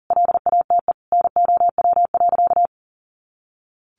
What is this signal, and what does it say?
Morse code, heard as a keyed tone, about 40 wpm, 700 Hz. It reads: LATE NOW.
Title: Sentences from Top 300 Words 40wpm